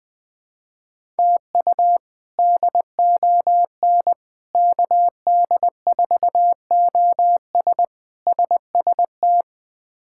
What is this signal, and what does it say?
Morse code, heard as a keyed tone, about 21 wpm, 700 Hz. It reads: TU DON KD4OS SST
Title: K1USN SST QSO Practice - 20wpm